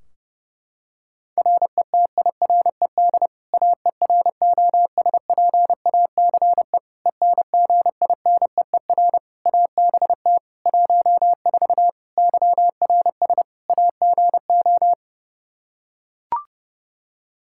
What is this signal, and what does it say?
Morse code, heard as a keyed tone, about 30 wpm, 700 Hz. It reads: RETIRED AEROSPACE ENGINEER ABT 14 YRS AGO
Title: K1USN SST QSO Practice 30wpm